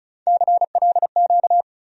Morse code, heard as a keyed tone, about 34 wpm, 700 Hz. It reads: CLQ
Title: USA Call Sign Postfixes - 3 Letters 35wpm